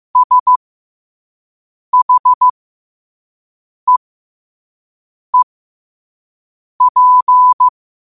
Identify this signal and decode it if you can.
Morse code, heard as a keyed tone, about 15 wpm, spaced as for 5 wpm, 1000 Hz. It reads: SHEEP
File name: qotdfiles / 5wpm